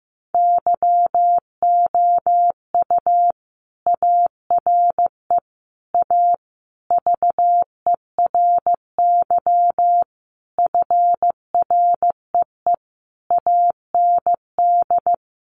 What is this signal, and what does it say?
Morse code, heard as a keyed tone, about 15 wpm, 700 Hz. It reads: YOU ARE A VERY FREE AND
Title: Sentences from Top 1000 Words 15wpm